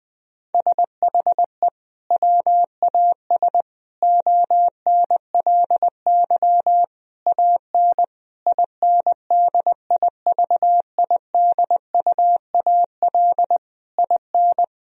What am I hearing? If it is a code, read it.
Morse code, heard as a keyed tone, about 20 wpm, 700 Hz. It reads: SHE WAS ONLY AN INDIVIDUAL IN
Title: Sentences from Top 1000 Words - Review Part 4 - 20wpm